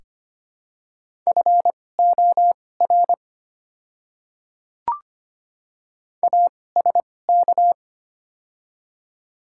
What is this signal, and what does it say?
Morse code, heard as a keyed tone, about 25 wpm, spaced as for 18 wpm, 700 Hz. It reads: FOR ASK
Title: 2x Character Spacing - Top 300 Words - 25wpm